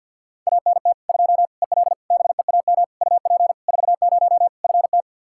Morse code, heard as a keyed tone, about 50 wpm, 700 Hz. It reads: ANT 2 EL BEAM UP 40 FT
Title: Practice QSO - Exchange 2 - Info - 50wpm